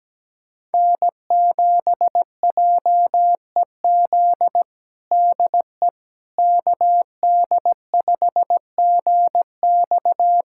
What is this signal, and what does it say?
Morse code, heard as a keyed tone, about 17 wpm, 700 Hz. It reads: N7JEZ DE KD5GX
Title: USA Callsign DE Callsign - 5 Characters - 17wpm